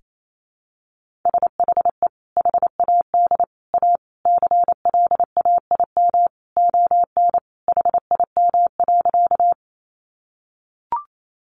Morse code, heard as a keyed tone, about 28 wpm, 700 Hz. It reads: SHE HAD A CLAIM ON HIM.
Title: Sentences from Top 1500 Words 28wpm